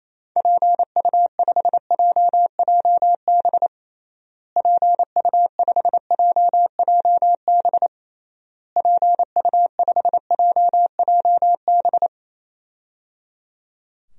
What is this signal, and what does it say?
Morse code, heard as a keyed tone, about 28 wpm, 700 Hz. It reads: PU5JJB PU5JJB PU5JJB
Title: DXCC-100 Call Signs - 6 Characters - Repeated 3x - 28wpm